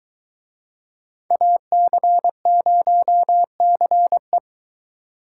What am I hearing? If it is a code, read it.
Morse code, heard as a keyed tone, about 22 wpm, 700 Hz. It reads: AC0CE